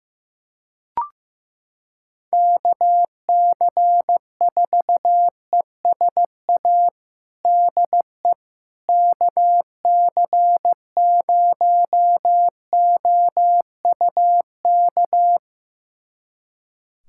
Morse code, heard as a keyed tone, about 15 wpm, 700 Hz. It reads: KC4ESA DE KC0OUK
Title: USA Callsign DE Callsign - 6 Characters - 15wpm